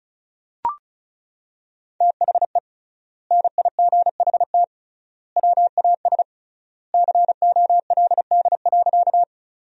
Morse code, heard as a keyed tone, about 34 wpm, 700 Hz. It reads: THE NIGHT WAS COLD.
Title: 3x Word-Spacing - Sentences from Top 400 Words - 35wpm